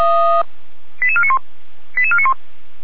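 Five-tone selective call (ZVEI), heard as quick runs of five tones, tones of about 70 ms. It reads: 88491 88491
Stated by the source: Five-tone ZVEI selective call received on an SDR.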